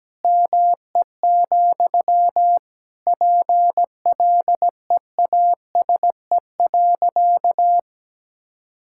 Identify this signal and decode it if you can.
Morse code, heard as a keyed tone, about 17 wpm, 700 Hz. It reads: ME, PLEASE.
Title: Sentences from Top 1600 Words 17wpm